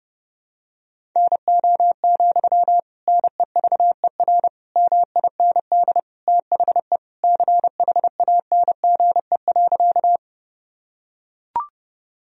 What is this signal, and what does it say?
Morse code, heard as a keyed tone, about 31 wpm, 700 Hz. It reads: NO, NEVER MIND THE CHANGE.
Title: Sentences from Top 1000 Words - Review Part 1 - 30wpm